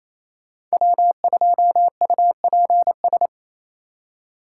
Morse code, heard as a keyed tone, about 28 wpm, 700 Hz. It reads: W2UPS
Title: USA Call Signs - 2x Character Spacing - 28wpm